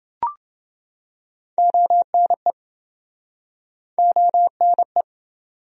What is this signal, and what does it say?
Morse code, heard as a keyed tone, about 29 wpm, 700 Hz. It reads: ONE ONE